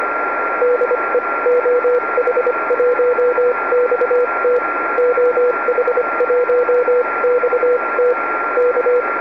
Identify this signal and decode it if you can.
Morse code, heard as a keyed tone, about 25 wpm, spaced as for 22 wpm, 490 Hz, with radio noise. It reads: DE OH1XT OH1XT K
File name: oh1xt-2m